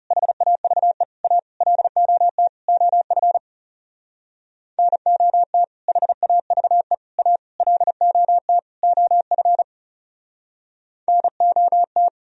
Morse code, heard as a keyed tone, about 35 wpm, 700 Hz. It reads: HAVE A LOT OF NOT HAVE A LOT OF NOT
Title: Speed-Racing - Sets of 5 Words 30wpm